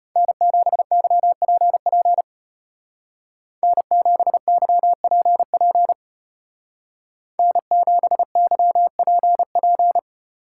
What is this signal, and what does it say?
Morse code, heard as a keyed tone, about 34 wpm, 700 Hz. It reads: N7YPP N7YPP N7YPP